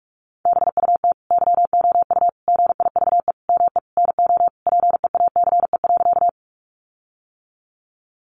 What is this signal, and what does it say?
Morse code, heard as a keyed tone, about 44 wpm, 700 Hz. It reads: BUT YOU GIVE ME NO PEACE.
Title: Sentences from Top 1500 Words 45wpm